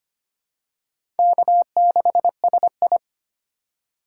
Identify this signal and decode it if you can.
Morse code, heard as a keyed tone, about 25 wpm, 700 Hz. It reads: K6SI
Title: USA Call Signs - 4 Characters - 3x Character Spacing - 25wpm